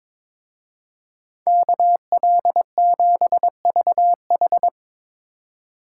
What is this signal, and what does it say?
Morse code, heard as a keyed tone, about 25 wpm, 700 Hz. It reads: KL7VH
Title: Speed Racing - USA 2x2 Call Signs 22wpm